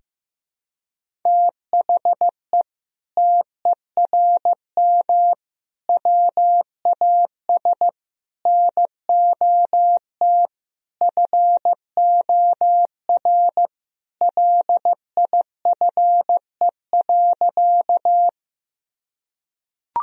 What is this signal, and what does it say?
Morse code, heard as a keyed tone, about 15 wpm, 700 Hz. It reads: THE TERM WAS NOT FOR LIFE.
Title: Sentences from Top 1000 Words - Review Part 4 - 15wpm